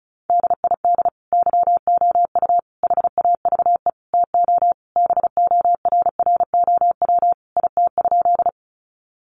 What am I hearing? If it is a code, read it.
Morse code, heard as a keyed tone, about 34 wpm, 700 Hz. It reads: DID YOU HAVE TO BORROW IT?
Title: Sentences from Top 1500 Words 35wpm